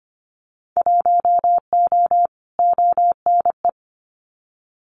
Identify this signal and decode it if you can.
Morse code, heard as a keyed tone, about 25 wpm, 700 Hz. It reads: 1O ONE
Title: Winter Field Day Exchanges 25wpm